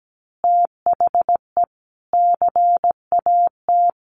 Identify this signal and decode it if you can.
Morse code, heard as a keyed tone, about 17 wpm, 700 Hz. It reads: THE CAT
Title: Additional Easy Phrases 17wpm